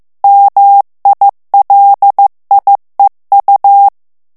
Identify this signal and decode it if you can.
Morse code, heard as a keyed tone, about 15 wpm, 800 Hz. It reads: MILIEU
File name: morse mp3 / FRANCAIS TEXTE CLAIR 2 VITESSE 14 MOTS MINUTES